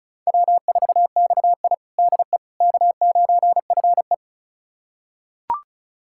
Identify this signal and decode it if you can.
Morse code, heard as a keyed tone, about 36 wpm, 700 Hz. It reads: W4XI DE K9FE
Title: USA Callsign DE Callsign - 4 Characters - 35wpm